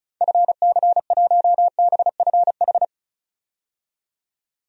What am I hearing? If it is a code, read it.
Morse code, heard as a keyed tone, about 36 wpm, 700 Hz. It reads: FC1BFH
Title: DXCC-100 Call Signs - 6 Characters - 35wpm